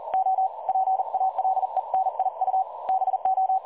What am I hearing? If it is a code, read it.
Morse code, heard as a keyed tone, about 38 wpm, 760 Hz. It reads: O PALENIU BO
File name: dlazaawansowanych / nagraniazpasma / websdr recording 3565.0kHz